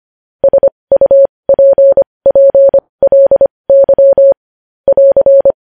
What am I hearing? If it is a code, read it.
Morse code, heard as a keyed tone, about 25 wpm, spaced as for 20 wpm, 550 Hz. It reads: SUPPLY <AR>
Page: news-headlines-in-morse-code-at-20-wpm